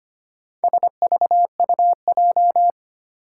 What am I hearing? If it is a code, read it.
Morse code, heard as a keyed tone, about 25 wpm, 700 Hz. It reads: SVUJ
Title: Four Challenging Letter-Number Combinations 25wpm